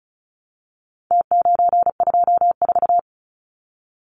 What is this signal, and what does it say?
Morse code, heard as a keyed tone, about 34 wpm, 700 Hz. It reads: T924
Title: Numbers - Contest Style - 4-Digit 35wpm